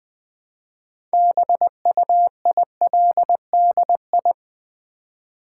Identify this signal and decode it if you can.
Morse code, heard as a keyed tone, about 20 wpm, 700 Hz. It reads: BUILDI
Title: Build A Word - 3 Letters At A Time - 8-Letter Words - 20wpm